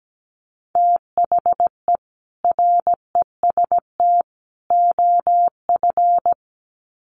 Morse code, heard as a keyed tone, about 17 wpm, 700 Hz. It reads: THE REST OF
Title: Sets of 3 Words 17wpm